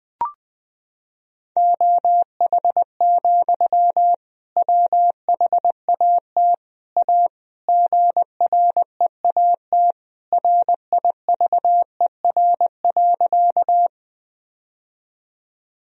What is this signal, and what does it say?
Morse code, heard as a keyed tone, about 20 wpm, 700 Hz. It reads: OH, WHAT A GREAT RIVER.
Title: Sentences from Top 1000 Words - Review Part 1 - 20wpm